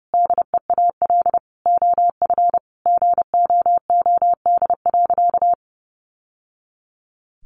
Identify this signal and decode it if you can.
Morse code, heard as a keyed tone, about 30 wpm, 700 Hz. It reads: DEAL OF GOOD.
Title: Sentences from Top 900 Words 30wpm